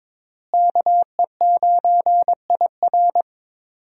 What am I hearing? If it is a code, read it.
Morse code, heard as a keyed tone, about 21 wpm, 700 Hz. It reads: KE9IR
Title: USA Call Signs - 5 Characters - Repeated 2x - 22wpm